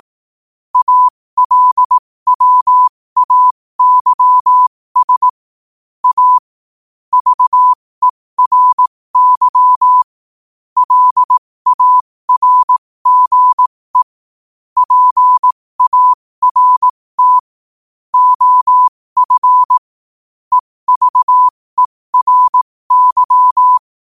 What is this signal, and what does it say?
Morse code, heard as a keyed tone, about 18 wpm, spaced as for 15 wpm, 1000 Hz. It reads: ALWAYS A VERY LARGE PART OF EVERY
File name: qotdfiles / 15wpm